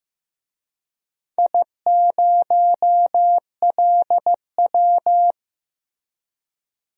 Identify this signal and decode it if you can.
Morse code, heard as a keyed tone, about 15 wpm, 700 Hz. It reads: I0LW
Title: DXCC-100 Call Signs - 4 Characters - Repeated 3x - 15wpm